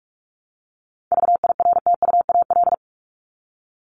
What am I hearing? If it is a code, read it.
Morse code, heard as a keyed tone, about 44 wpm, 700 Hz. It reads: VIRTUAL
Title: LY Suffix - Top 100 Words with Base Words 45wpm